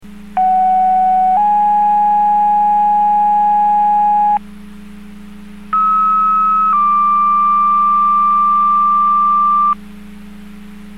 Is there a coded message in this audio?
Yes: two-tone page from 0.4 to 9.7 s, heard as a one-second tone, then a three-second tone, about 750 then 830 Hz, about 1290 then 1180 Hz.